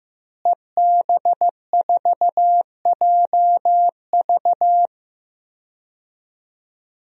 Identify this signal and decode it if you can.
Morse code, heard as a keyed tone, about 15 wpm, 700 Hz. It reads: EB4JV
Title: DXCC-100 Call Signs - 5 Characters - Repeated 3x - 15wpm